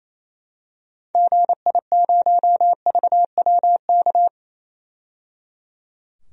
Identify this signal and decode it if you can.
Morse code, heard as a keyed tone, about 28 wpm, 700 Hz. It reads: GI0VWK
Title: DXCC-100 Call Signs - 6 Characters - 28wpm